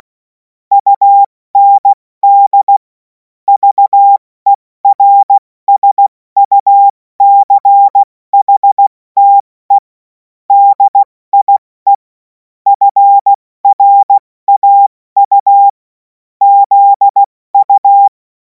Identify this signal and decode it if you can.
Morse code, heard as a keyed tone, about 16 wpm, spaced as for 14 wpm, 800 Hz. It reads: UND VERSUCHTE DIE FRAU ZU
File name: Morsekurs / Witze / Witz 03. Februar 2026 70 F80